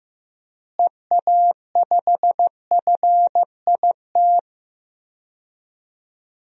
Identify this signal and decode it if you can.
Morse code, heard as a keyed tone, about 15 wpm, 700 Hz. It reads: EA5FIT
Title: DXCC-100 Call Signs - 6 Characters - Repeated 3x - 15wpm